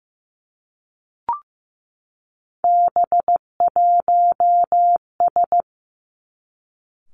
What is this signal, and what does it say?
Morse code, heard as a keyed tone, about 15 wpm, 700 Hz. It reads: B1S
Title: Three Challenging Letter-Number Combinations 15wpm